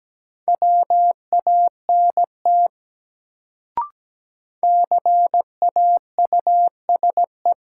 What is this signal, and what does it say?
Morse code, heard as a keyed tone, about 17 wpm, 700 Hz. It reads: WANT CAUSE
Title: Sentences from Top 200 Words - New Word Drill - 17wpm